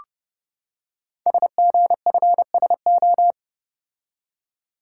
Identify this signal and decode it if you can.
Morse code, heard as a keyed tone, about 30 wpm, 700 Hz. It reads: SGFSO
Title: Code Groups - 5 Letters - 30wpm